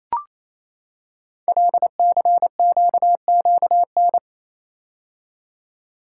Morse code, heard as a keyed tone, about 29 wpm, 700 Hz. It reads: LCQQN
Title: Code Groups - 5 Letters - 28wpm